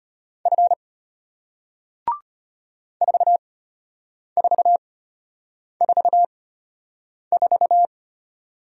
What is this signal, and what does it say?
Morse code, heard as a keyed tone, about 34 wpm, spaced as for 8 wpm, 690 Hz. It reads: F 4444